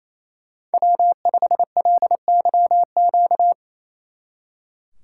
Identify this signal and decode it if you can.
Morse code, heard as a keyed tone, about 34 wpm, 700 Hz. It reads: W5LYQ
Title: Speed Racing - USA Call Signs 28wpm